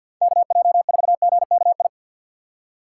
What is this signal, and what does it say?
Morse code, heard as a keyed tone, about 52 wpm, 700 Hz. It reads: KJ4GKI
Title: USA Call Signs - 6 Characters - 8x Character Spacing - 50wpm